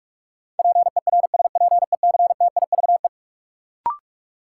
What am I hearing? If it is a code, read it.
Morse code, heard as a keyed tone, about 47 wpm, 700 Hz. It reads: PERSPECTIVE
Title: Sentences from Top 1200 Words - New Word Drill Alt - 45wpm